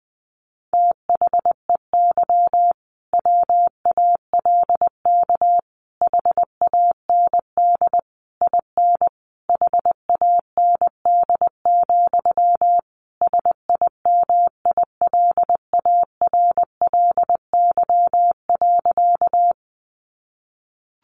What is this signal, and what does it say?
Morse code, heard as a keyed tone, about 20 wpm, 700 Hz. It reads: THEY WALK HAND IN HAND, SIMILARLY.
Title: Sentences from Top 1400 Words 20wpm